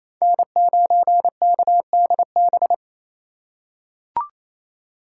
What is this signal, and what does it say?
Morse code, heard as a keyed tone, about 28 wpm, 700 Hz. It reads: N9KDB
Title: K1USN SST QSO Practice - 28wpm